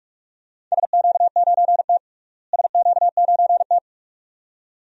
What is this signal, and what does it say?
Morse code, heard as a keyed tone, about 46 wpm, 700 Hz. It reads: SQ9T SQ9T
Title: DXCC-100 Call Signs - 4 Characters - Repeated 2x - 45wpm